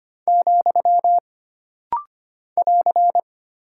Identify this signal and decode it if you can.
Morse code, heard as a keyed tone, about 25 wpm, 700 Hz. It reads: , <AR>
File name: Single Punctuation-Prosign - Rapid-Fire - 25wpm